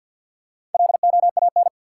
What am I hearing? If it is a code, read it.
Morse code, heard as a keyed tone, about 49 wpm, 700 Hz. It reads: LOAN
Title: Four Letter Words - 50wpm